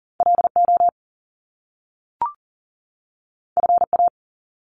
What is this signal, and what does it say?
Morse code, heard as a keyed tone, about 41 wpm, 700 Hz. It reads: LO FA
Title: Top Two Letter Combinations in Words 40wpm